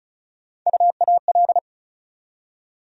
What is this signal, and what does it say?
Morse code, heard as a keyed tone, about 36 wpm, 700 Hz. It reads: UAL
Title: USA Call Sign Postfixes - 3 Letters 35wpm